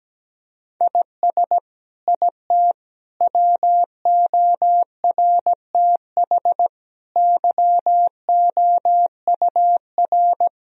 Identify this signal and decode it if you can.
Morse code, heard as a keyed tone, about 17 wpm, 700 Hz. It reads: IS IT WORTH YOUR
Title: Sentences from Top 1000 Words - Review Part 1 - 17wpm